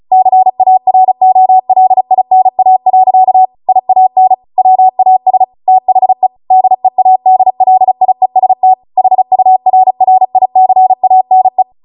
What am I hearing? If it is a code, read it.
Morse code, heard as a keyed tone, about 36 wpm, 750 Hz. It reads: CAROLINA. IAN WAS THE DEADLIEST HURRICANE